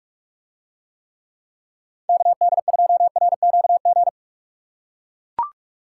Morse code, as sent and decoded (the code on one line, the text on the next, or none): -.- -.. ..--- .-. --.- --.
KD2RQG